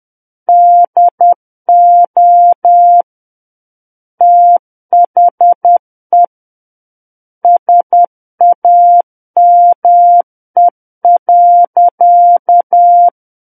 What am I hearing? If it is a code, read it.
Morse code, heard as a keyed tone, about 10 wpm, 700 Hz. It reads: DO THE SAME.
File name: Book One / 01-Chapter 1 - 10 WPM